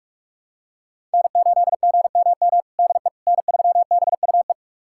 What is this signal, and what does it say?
Morse code, heard as a keyed tone, about 46 wpm, 700 Hz. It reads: N8GMM DE N3BUE
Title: USA Callsign DE Callsign - 5 Characters - 45wpm